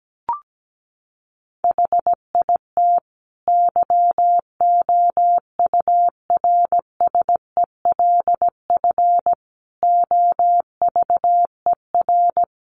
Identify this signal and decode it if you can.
Morse code, heard as a keyed tone, about 17 wpm, 700 Hz. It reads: HIT YOURSELF OVER
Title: Sentences from Top 1200 Words 17wpm